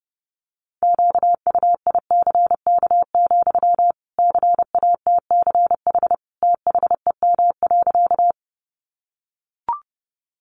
Morse code, heard as a keyed tone, about 30 wpm, 700 Hz. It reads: QUICK, CATCH THEM.
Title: Sentences from Top 1500 Words 30wpm